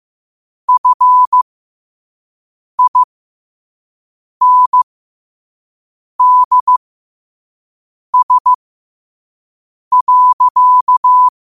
Morse code, heard as a keyed tone, about 15 wpm, spaced as for 5 wpm, 1000 Hz. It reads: FINDS.
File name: qotdfiles / 5wpm